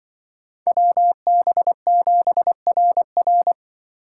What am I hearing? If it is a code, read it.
Morse code, heard as a keyed tone, about 24 wpm, 700 Hz. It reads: WB7RR
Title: Speed Racing - USA 2x2 Call Signs 20wpm